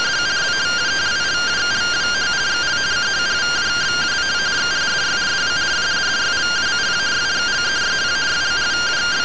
Two-tone teleprinter signal, about 170 Hz shift.